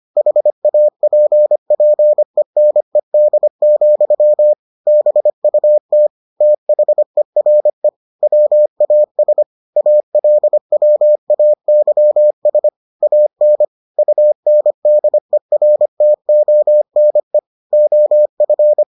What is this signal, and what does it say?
Morse code, heard as a keyed tone, about 25 wpm, 600 Hz. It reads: HAPPENED, BUT THERE WAS ALWAYS AN UNDERTONE OF